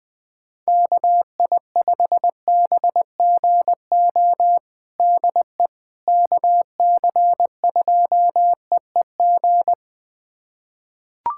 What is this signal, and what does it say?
Morse code, heard as a keyed tone, about 21 wpm, 700 Hz. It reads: KI5BGO DE KC2EEG
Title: USA Callsign DE Callsign 20wpm